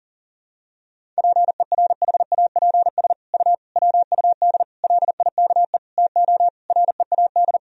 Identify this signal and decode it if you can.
Morse code, heard as a keyed tone, about 40 wpm, 700 Hz. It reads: PERHAPS U WUD LIKE TO READ
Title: Sentences from Top 500 Words and Abbreviations 40wpm